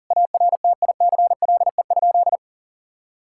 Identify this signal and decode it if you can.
Morse code, heard as a keyed tone, about 37 wpm, 700 Hz. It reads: ARTICLE?
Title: Sentences from Top 1000 Words - Review Part 5 - 40wpm